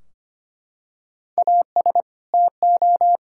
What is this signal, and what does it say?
Morse code, heard as a keyed tone, about 25 wpm, 700 Hz. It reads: AS TO
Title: Sentences from Top 1600 Words 25wpm